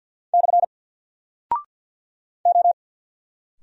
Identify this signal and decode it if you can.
Morse code, heard as a keyed tone, about 50 wpm, 700 Hz. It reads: / O
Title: Single Character (Letter:Number:Punctuation:Prosign) 50wpm